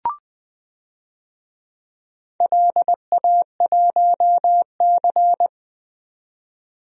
Morse code, heard as a keyed tone, about 21 wpm, 700 Hz. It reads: LA1C
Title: DXCC-100 Call Signs - 4 Characters - 20wpm